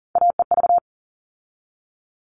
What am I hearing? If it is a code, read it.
Morse code, heard as a keyed tone, about 40 wpm, 700 Hz. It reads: AEV
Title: All Cut Numbers - 3-Digit 40wpm